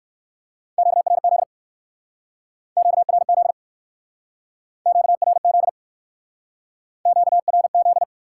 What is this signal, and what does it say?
Morse code, heard as a keyed tone, about 59 wpm, 700 Hz. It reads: QRZ QRZ QRZ QRZ